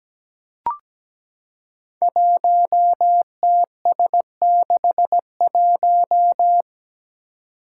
Morse code, heard as a keyed tone, about 17 wpm, 700 Hz. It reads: 1TS61
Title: Five Challenging Letter-Number Combinations 17wpm